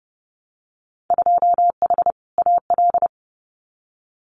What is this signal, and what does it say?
Morse code, heard as a keyed tone, about 30 wpm, 700 Hz. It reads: 2H AL
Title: Winter Field Day Exchanges 30wpm